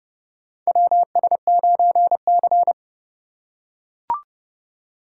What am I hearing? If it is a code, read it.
Morse code, heard as a keyed tone, about 30 wpm, 700 Hz. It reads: WS9C
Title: USA Call Signs - 4 Characters - 4x Character Spacing - 30wpm